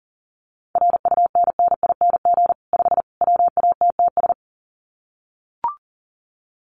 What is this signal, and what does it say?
Morse code, heard as a keyed tone, about 37 wpm, 700 Hz. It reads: RUNNING 5 WATTS
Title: Common QSO Phrases 40wpm